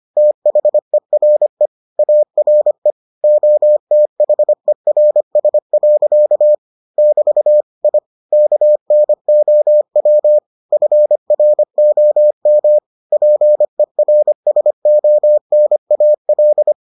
Morse code, heard as a keyed tone, about 25 wpm, 600 Hz. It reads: THERE ARE OTHERS. <BT> I KNOW FROM PERSONAL